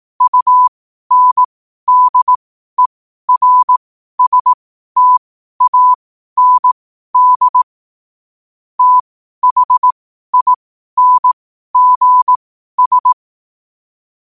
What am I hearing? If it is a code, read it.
Morse code, heard as a keyed tone, about 18 wpm, spaced as for 12 wpm, 1000 Hz. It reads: UNDERSTAND THINGS
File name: qotdfiles / 12wpm